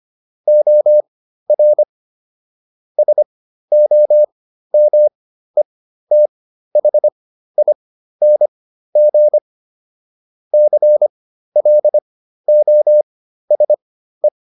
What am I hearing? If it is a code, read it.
Morse code, heard as a keyed tone, about 25 wpm, spaced as for 13 wpm, 600 Hz. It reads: OR SOMETHING CLOSE